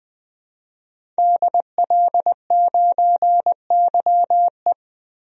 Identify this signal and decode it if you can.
Morse code, heard as a keyed tone, about 20 wpm, 700 Hz. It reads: DL9YE
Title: DXCC-100 Call Signs - 5 Characters - 20wpm